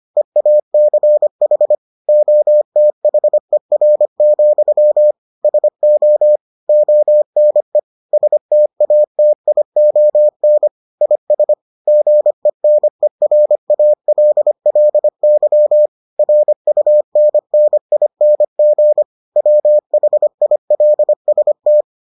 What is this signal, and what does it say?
Morse code, heard as a keyed tone, about 25 wpm, 600 Hz. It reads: EACH OTHER, SO ONE STATION IS GENERALLY RUNNING WHILST